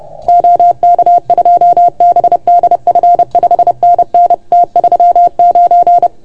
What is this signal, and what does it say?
Morse code, heard as a keyed tone, about 30 wpm, 690 Hz. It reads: OK2BDF 5NN T39
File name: Archiv / CUC381 / OK1FGD CUC 381